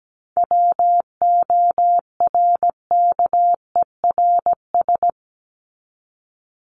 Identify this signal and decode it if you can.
Morse code, heard as a keyed tone, about 17 wpm, 700 Hz. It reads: WORKERS
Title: ES:S Suffix - Top 100 Words with Base Words 17wpm